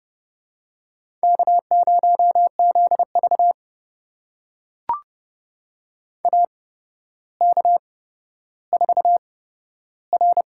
Morse code, heard as a keyed tone, about 32 wpm, spaced as for 13 wpm, 700 Hz. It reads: K0ZV AK4R
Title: USA Call Signs - 4 Characters - 8x Character Spacing - 30wpm